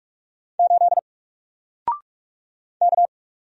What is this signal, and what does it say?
Morse code, heard as a keyed tone, about 44 wpm, 700 Hz. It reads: 8 K
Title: Single Character (Letter:Number:Punctuation:Prosign) 45wpm